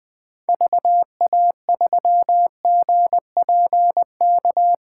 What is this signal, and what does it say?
Morse code, heard as a keyed tone, about 20 wpm, 700 Hz. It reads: VA3GPK
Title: 2023 POTA Activator Call Signs - 20wpm